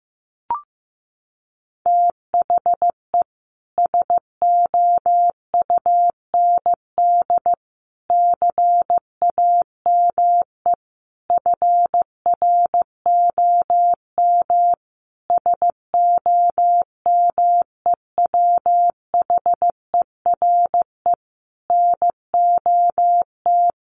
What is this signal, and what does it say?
Morse code, heard as a keyed tone, about 15 wpm, 700 Hz. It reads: THE SOUND CAME FROM SOMEWHERE NOT
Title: Sentences from Top 1400 Words 15wpm